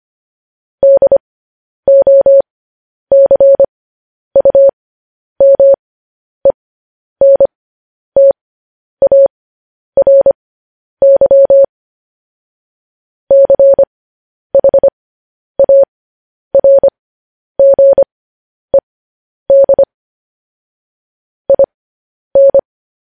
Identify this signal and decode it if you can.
Morse code, heard as a keyed tone, about 25 wpm, spaced as for 10 wpm, 550 Hz. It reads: DOCUMENTARY CHARGED IN
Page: news headlines in morse code at 10 wpm